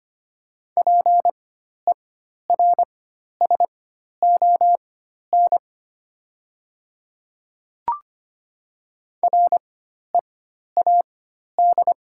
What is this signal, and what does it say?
Morse code, heard as a keyed tone, about 25 wpm, spaced as for 12 wpm, 700 Hz. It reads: PERSON READ
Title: 4x Character Spacing - Top 500 Words - 25wpm